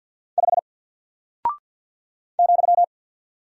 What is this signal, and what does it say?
Morse code, heard as a keyed tone, about 49 wpm, 700 Hz. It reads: 5 ,
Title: Single Character (Letter:Number:Punctuation:Prosign) Alt 50wpm